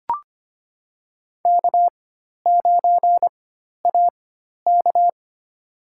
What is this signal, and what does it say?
Morse code, heard as a keyed tone, about 25 wpm, spaced as for 12 wpm, 700 Hz. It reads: K9AK
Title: USA Call Signs - 4 Characters - 4x Character Spacing - 25wpm